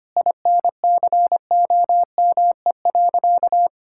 Morse code, heard as a keyed tone, about 25 wpm, 700 Hz. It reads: INCOME.
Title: Sentences from Top 1000 Words - Review Part 6 - 25wpm